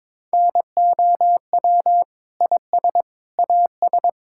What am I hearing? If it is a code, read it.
Morse code, heard as a keyed tone, about 22 wpm, 700 Hz. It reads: NOW IS AS
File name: Sentences with 9-Word Phrases - 22wpm